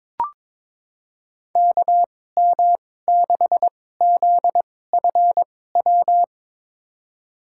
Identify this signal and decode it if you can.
Morse code, heard as a keyed tone, about 22 wpm, spaced as for 16 wpm, 700 Hz. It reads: KM6ZFW
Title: USA Call Signs - 6 Characters - 2x Character Spacing - 22wpm